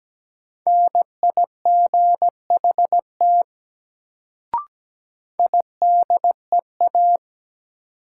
Morse code, heard as a keyed tone, about 17 wpm, 700 Hz. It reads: NIGHT IDEA
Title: Sentences from Top 300 Words - New Word Drill - 17wpm